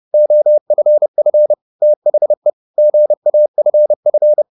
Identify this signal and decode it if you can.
Morse code, heard as a keyed tone, about 30 wpm, 600 Hz. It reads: OFF THE GAFF
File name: the old man and the sea / the old man and the sea 1 30wpm